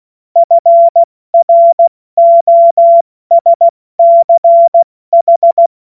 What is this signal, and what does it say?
Morse code, heard as a keyed tone, about 16 wpm, 680 Hz. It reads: FROSCH
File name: Morsekurs / Witze / Witz 08. November 2022 70 F80